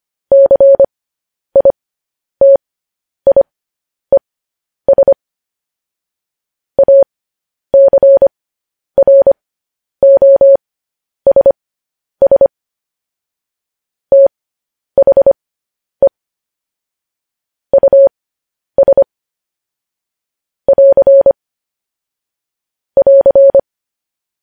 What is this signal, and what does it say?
Morse code, heard as a keyed tone, about 25 wpm, spaced as for 10 wpm, 550 Hz. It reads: CITIES ACROSS THE US <AR> <AR>
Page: news headlines in morse code at 10 wpm